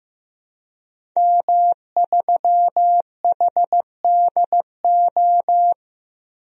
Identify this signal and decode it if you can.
Morse code, heard as a keyed tone, about 15 wpm, 700 Hz. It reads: M3HDO
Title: DXCC-100 Call Signs - 5 Characters - 15wpm